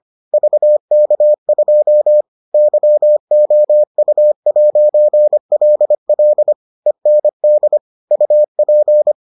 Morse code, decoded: VK2 YOU'LL END UP

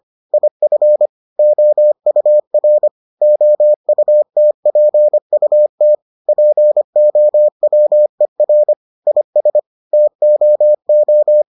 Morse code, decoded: IF OUR OUTPUT POWER IS TOO

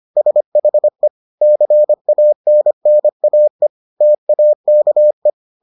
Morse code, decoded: SHE CANNAE TAKE

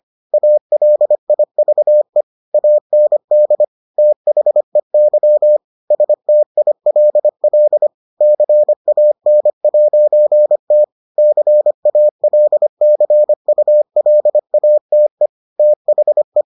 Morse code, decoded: ALIVE AND THEY STILL CAN'T CALCULATE THE

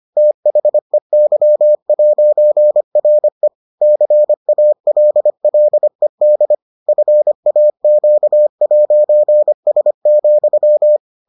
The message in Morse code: THEY'RE CALLED FAQ'S,

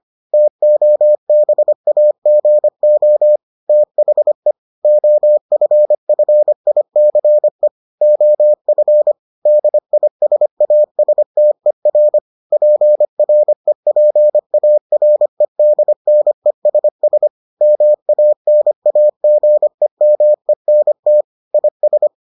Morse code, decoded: TOBAGO THE OFFICE OF DISASTER PREPAREDNESS MANAGEMENT IS

